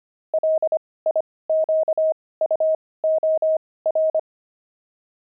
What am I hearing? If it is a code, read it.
Morse code, heard as a keyed tone, about 25 wpm, spaced as for 18 wpm, 630 Hz. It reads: LIQUOR
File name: pangram-fs-18-cs-25-fuji